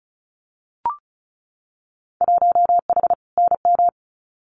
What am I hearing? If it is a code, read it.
Morse code, heard as a keyed tone, about 34 wpm, 700 Hz. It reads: 1H NM
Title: Winter Field Day Exchanges 35wpm